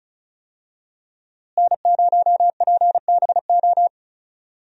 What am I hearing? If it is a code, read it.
Morse code, heard as a keyed tone, about 34 wpm, 700 Hz. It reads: N0PBO
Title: USA Call Signs - 5 Characters - Repeated 2x - 35wpm